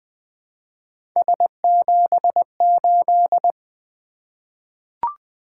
Morse code, heard as a keyed tone, about 20 wpm, 700 Hz. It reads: S78
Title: Three Letter-Number Combinations - 20wpm